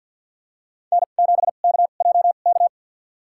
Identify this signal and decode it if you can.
Morse code, heard as a keyed tone, about 51 wpm, spaced as for 36 wpm, 700 Hz. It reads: N7KJK
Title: USA Call Signs - 5 Characters - 2x Character Spacing - 50wpm